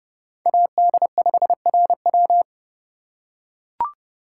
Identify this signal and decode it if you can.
Morse code, heard as a keyed tone, about 30 wpm, 700 Hz. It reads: AD5RW
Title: USA Call Signs - 5 Characters - 4x Character Spacing - 30wpm